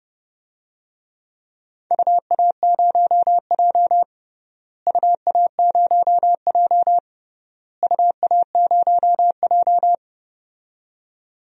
Morse code, heard as a keyed tone, about 29 wpm, 700 Hz. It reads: UA0J UA0J UA0J
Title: DXCC-100 Call Signs - 4 Characters - Repeated 3x - 30wpm